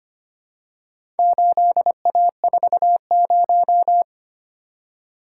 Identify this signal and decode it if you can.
Morse code, heard as a keyed tone, about 25 wpm, 700 Hz. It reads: 8A40
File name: practice / 25wpm / Four Letter-Number Combinations - 25wpm